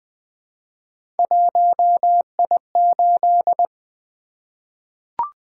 Morse code, heard as a keyed tone, about 20 wpm, 700 Hz. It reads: 1I8
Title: Three Letter-Number Combinations - 20wpm